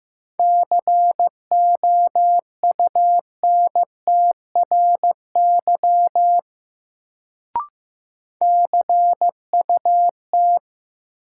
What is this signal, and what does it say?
Morse code, heard as a keyed tone, about 15 wpm, 700 Hz. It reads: COUNTRY CUT
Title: Sentences from Top 300 Words - New Word Drill - 15wpm